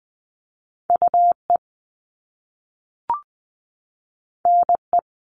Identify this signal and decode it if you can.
Morse code, heard as a keyed tone, about 20 wpm, 700 Hz. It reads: UE NE
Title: Top Two Letter Combinations in Words 20wpm